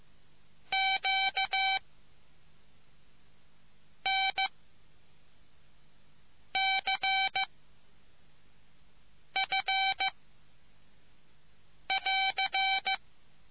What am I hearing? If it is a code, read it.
Morse code, heard as a keyed tone, about 15 wpm, 770 Hz. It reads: Q N C F <AR>